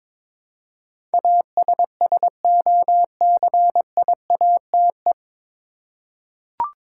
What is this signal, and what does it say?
Morse code, heard as a keyed tone, about 22 wpm, 700 Hz. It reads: ASSOCIATE
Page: Morse Code Ninja practice 22wpm